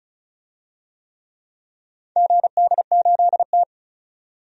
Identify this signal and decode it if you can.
Morse code, heard as a keyed tone, about 36 wpm, 700 Hz. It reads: GD8T
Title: DXCC-100 Call Signs - 4 Characters - 35wpm